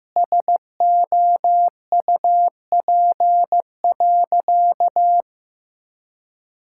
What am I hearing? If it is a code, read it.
Morse code, heard as a keyed tone, about 15 wpm, 700 Hz. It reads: SOUP.
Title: Sentences from Top 1000 Words - Review Part 6 - 15wpm